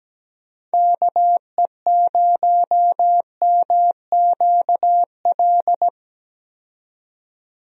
Morse code, heard as a keyed tone, about 17 wpm, 700 Hz. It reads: KE0MQL
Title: USA Call Signs - 6 Characters - 17wpm